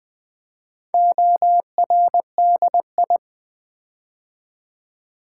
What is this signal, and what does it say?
Morse code, heard as a keyed tone, about 20 wpm, 700 Hz. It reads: ORDI
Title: Build A Word - 4 Letters At A Time - 8-Letter Words - 20wpm